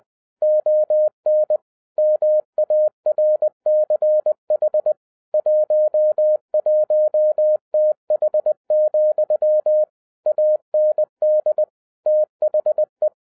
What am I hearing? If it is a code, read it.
Morse code, heard as a keyed tone, about 20 wpm, 610 Hz. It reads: ON MARCH 11TH, AND THE
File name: 2011-10-mp3 / n03-TARO JR0QWW 20wpm